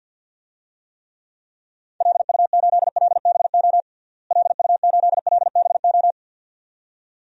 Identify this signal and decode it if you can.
Morse code, heard as a keyed tone, about 53 wpm, 700 Hz. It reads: LU8LBO LU8LBO
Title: DXCC-100 Call Signs - 6 Characters - Repeated 2x - 50wpm